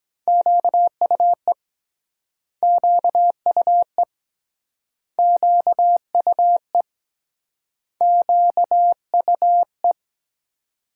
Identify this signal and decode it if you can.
Morse code, heard as a keyed tone, about 24 wpm, 700 Hz. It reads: QUE QUE QUE QUE